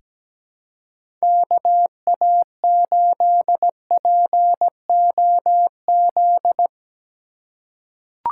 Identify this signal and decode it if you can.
Morse code, heard as a keyed tone, about 17 wpm, 700 Hz. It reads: KA8POZ
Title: K1USN SST QSO Practice - 17wpm